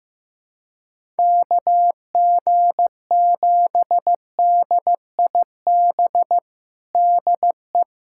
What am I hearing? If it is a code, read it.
Morse code, heard as a keyed tone, about 15 wpm, 700 Hz. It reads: KG7DIB DE
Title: USA Callsign DE Callsign 15wpm